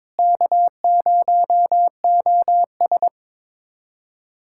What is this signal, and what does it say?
Morse code, heard as a keyed tone, about 22 wpm, 700 Hz. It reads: K0OS
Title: USA Callsign DE Callsign - 4 Characters - 22wpm